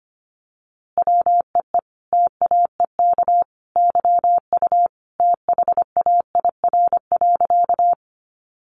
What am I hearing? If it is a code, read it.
Morse code, heard as a keyed tone, about 25 wpm, 700 Hz. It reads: WEE TAEK YU THAIR.
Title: A Sound Of Thunder 25wpm